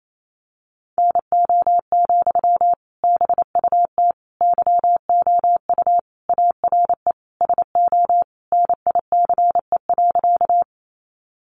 Morse code, heard as a keyed tone, about 28 wpm, 700 Hz. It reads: NO, BUT YOU ARE SO NICE.
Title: Sentences from Top 1100 Words 28wpm